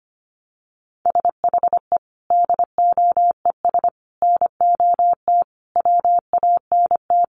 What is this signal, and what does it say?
Morse code, heard as a keyed tone, about 25 wpm, 700 Hz. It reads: SHE DOES NOT WANT
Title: Additional Easy Phrases 25wpm